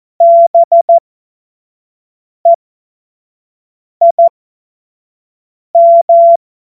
Morse code, heard as a keyed tone, about 14 wpm, spaced as for 5 wpm, 680 Hz. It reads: BEIM
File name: Morsekurs / Witze / Witz 31.Oktober 2023 25 F70